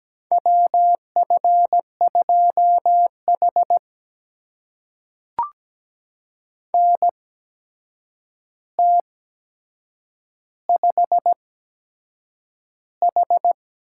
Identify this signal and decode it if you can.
Morse code, heard as a keyed tone, about 17 wpm, spaced as for 7 wpm, 700 Hz. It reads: WF2H NT5H
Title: USA Call Signs - 4 Characters - 8x Character Spacing - 17wpm